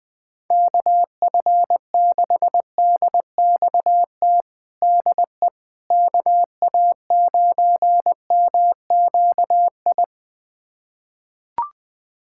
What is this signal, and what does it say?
Morse code, heard as a keyed tone, about 20 wpm, 700 Hz. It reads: KF6DXT DE KA9MQI
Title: USA Callsign DE Callsign - 6 Characters - 20wpm